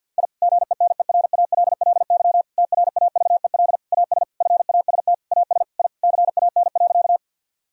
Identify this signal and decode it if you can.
Morse code, heard as a keyed tone, about 50 wpm, 700 Hz. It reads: I GENERALLY TRAVEL AS FAST AS I CAN.
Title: Sentences from Top 1000 Words - Review Part 4 - 50wpm